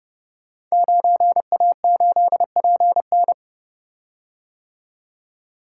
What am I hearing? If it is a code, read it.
Morse code, heard as a keyed tone, about 29 wpm, 700 Hz. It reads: 9A8PN
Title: Code Groups - 5 Letter:Numbers - 30wpm